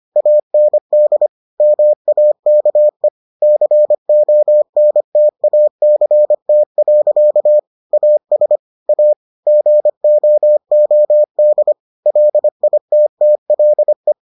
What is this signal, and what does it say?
Morse code, heard as a keyed tone, about 25 wpm, 600 Hz. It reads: AND MAKE CONTACT. AS A GOOD LITTLE